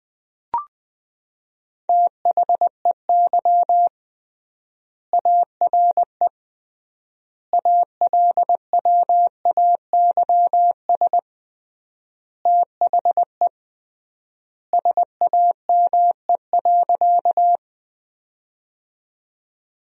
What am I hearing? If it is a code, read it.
Morse code, heard as a keyed tone, about 21 wpm, 700 Hz. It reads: THEY ARE ALWAYS THE SAME.
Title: 3x Word-Spacing - Sentences from Top 300 Words - 20wpm